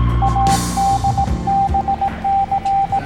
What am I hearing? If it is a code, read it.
Morse code, heard as a keyed tone, about 18 wpm, 790 Hz, with noise. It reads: ADBC
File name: jingles / Kapitaen Nemo 08